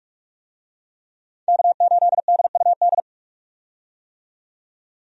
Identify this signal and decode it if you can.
Morse code, heard as a keyed tone, about 41 wpm, 700 Hz. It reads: K8DUD
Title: USA Call Signs - 5 Characters - 45wpm